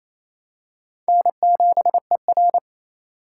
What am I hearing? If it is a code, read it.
Morse code, heard as a keyed tone, about 28 wpm, 700 Hz. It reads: N7ER
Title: USA Call Signs - 4 Characters - 4x Character Spacing - 28wpm